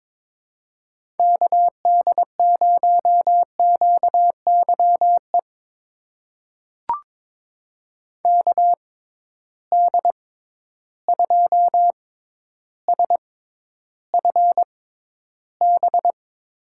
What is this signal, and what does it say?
Morse code, heard as a keyed tone, about 22 wpm, spaced as for 11 wpm, 700 Hz. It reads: KD0QYE KD2SFB
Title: USA Call Signs - 6 Characters - 6x Character Spacing - 22wpm